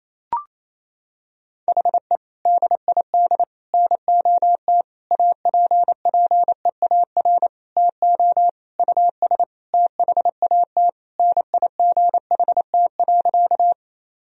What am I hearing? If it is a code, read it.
Morse code, heard as a keyed tone, about 28 wpm, 700 Hz. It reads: HE DID NOT APPEAR TO US THAT NIGHT.
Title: Sentences from Top 1000 Words - Review Part 2 - 28wpm